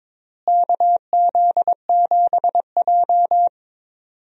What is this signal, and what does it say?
Morse code, heard as a keyed tone, about 22 wpm, 700 Hz. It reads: KZ7J
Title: USA Call Signs - 4 Characters - Repeated 3x - 22wpm